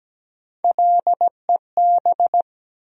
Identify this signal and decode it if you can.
Morse code, heard as a keyed tone, about 17 wpm, 700 Hz. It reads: LEB
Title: USA Call Sign Postfixes - 3 Letters 17wpm